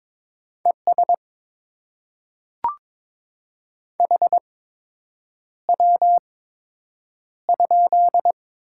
Morse code, heard as a keyed tone, about 22 wpm, spaced as for 8 wpm, 700 Hz. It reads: ES HW?
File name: Top 20 QSO Elements - 8x Character Spacing - 22wpm